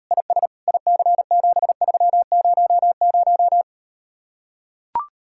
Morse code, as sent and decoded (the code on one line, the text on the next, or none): .. ... / .. -.-. --... ...-- ----- -----
IS IC7300